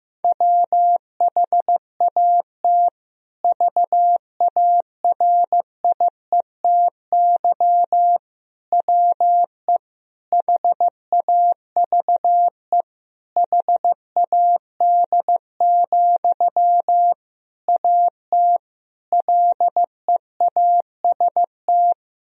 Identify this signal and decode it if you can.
Morse code, heard as a keyed tone, about 15 wpm, 700 Hz. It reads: WHAT VARIETY WE HAVE HAD, AT LEAST
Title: Sentences from Top 1000 Words - Review Part 4 - 15wpm